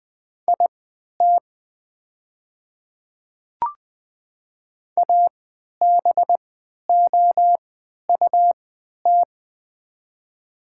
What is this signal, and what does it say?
Morse code, heard as a keyed tone, about 20 wpm, spaced as for 11 wpm, 700 Hz. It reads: IT ABOUT
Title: 3x Character Spacing - Top 100 Words - 20wpm